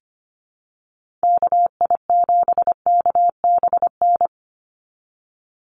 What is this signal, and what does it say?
Morse code, heard as a keyed tone, about 25 wpm, 700 Hz. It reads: KI7KBN
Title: USA Call Signs 25wpm